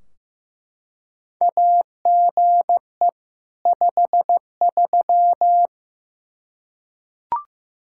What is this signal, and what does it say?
Morse code, heard as a keyed tone, about 15 wpm, 700 Hz. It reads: AGE 53
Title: Practice QSO - Exchange 3 - More Info - 15wpm